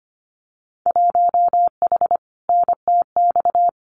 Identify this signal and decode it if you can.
Morse code, heard as a keyed tone, about 25 wpm, 700 Hz. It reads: 1H NTX
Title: Winter Field Day Exchanges 25wpm